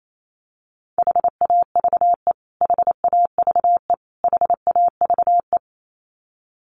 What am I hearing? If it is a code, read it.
Morse code, heard as a keyed tone, about 28 wpm, 700 Hz. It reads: HAVE HAVE HAVE
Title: Top 200 Words Repeated 28wpm